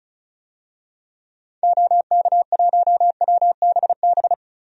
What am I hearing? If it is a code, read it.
Morse code, heard as a keyed tone, about 36 wpm, 700 Hz. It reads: OK1WBB
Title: DXCC-100 Call Signs - 6 Characters - Repeated 2x - 35wpm